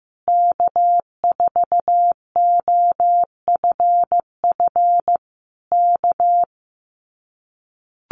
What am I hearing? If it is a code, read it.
Morse code, heard as a keyed tone, about 15 wpm, 700 Hz. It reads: K4OFF K
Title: Common QSO Phrases Encore 15wpm